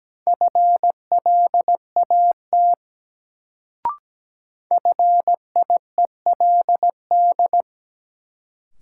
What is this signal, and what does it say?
Morse code, heard as a keyed tone, about 17 wpm, 700 Hz. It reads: FLAT FIELD
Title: Sentences from Top 1000 Words - New Word Drill - 17wpm